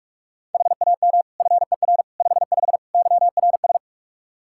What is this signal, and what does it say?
Morse code, heard as a keyed tone, about 46 wpm, 700 Hz. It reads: HAM FER 55 YRS
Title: Practice QSO - Exchange 3 - More Info - 45wpm